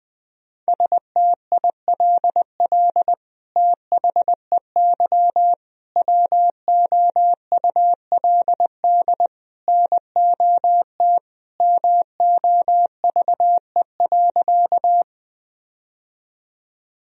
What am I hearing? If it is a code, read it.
Morse code, heard as a keyed tone, about 20 wpm, 700 Hz. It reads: STILL THEY WOULD NOT MOVE.
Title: Sentences from Top 1000 Words - Review Part 1 - 20wpm